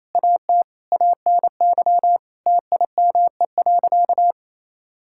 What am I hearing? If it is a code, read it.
Morse code, heard as a keyed tone, about 28 wpm, 700 Hz. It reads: AT ANY TIME.
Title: Sentences from Top 1000 Words - Review Part 6 - 28wpm